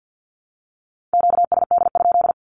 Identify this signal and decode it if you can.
Morse code, heard as a keyed tone, about 50 wpm, 700 Hz. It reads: QSD?
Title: Q-Codes Repeated 3x 50wpm